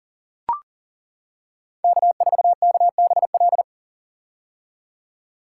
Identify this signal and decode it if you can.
Morse code, heard as a keyed tone, about 38 wpm, 700 Hz. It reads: K4KBL
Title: K1USN SST QSO Practice - 40wpm